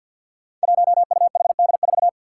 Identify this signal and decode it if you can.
Morse code, heard as a keyed tone, about 47 wpm, 700 Hz. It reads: 1UHD4
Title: Five Challenging Letter-Number Combinations 50wpm